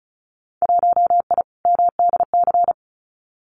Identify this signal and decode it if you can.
Morse code, heard as a keyed tone, about 34 wpm, 700 Hz. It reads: 1I MDC
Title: Winter Field Day Exchanges 35wpm